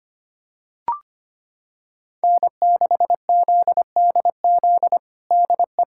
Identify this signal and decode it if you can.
Morse code, heard as a keyed tone, about 25 wpm, 700 Hz. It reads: N6ZDZ DE
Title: USA Callsign DE Callsign - 5 Characters - 25wpm